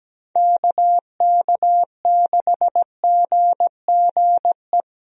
Morse code, heard as a keyed tone, about 17 wpm, 700 Hz. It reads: KK6GGE
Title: USA Call Signs - 6 Characters - Repeated 2x - 17wpm